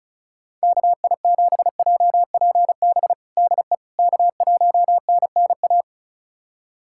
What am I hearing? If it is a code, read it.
Morse code, heard as a keyed tone, about 36 wpm, 700 Hz. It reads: KI7JPB DE K1NNA
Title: USA Callsign DE Callsign 35wpm